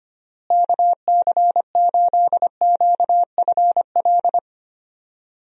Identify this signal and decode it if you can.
Morse code, heard as a keyed tone, about 25 wpm, 700 Hz. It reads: KC8QFL
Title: USA Call Signs - 6 Characters - Repeated 3x - 25wpm